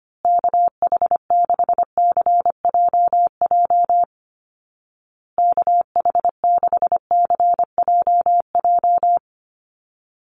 Morse code, heard as a keyed tone, about 25 wpm, 700 Hz. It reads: KH6CJJ KH6CJJ
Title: Call Signs Repeated 2x - 25wpm